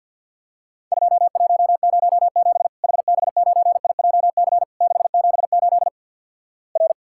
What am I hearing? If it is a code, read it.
Morse code, heard as a keyed tone, about 50 wpm, 700 Hz. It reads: 2107 HB9IJC 678 R
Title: Worked All Europe DX Contest - QTC Exchange with Repeats - 50wpm